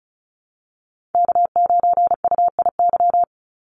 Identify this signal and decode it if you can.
Morse code, heard as a keyed tone, about 34 wpm, 700 Hz. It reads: K9UIY
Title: Call Signs Repeated 2x - 35wpm